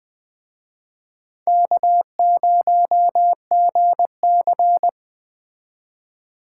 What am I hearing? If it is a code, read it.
Morse code, heard as a keyed tone, about 20 wpm, 700 Hz. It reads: K0GC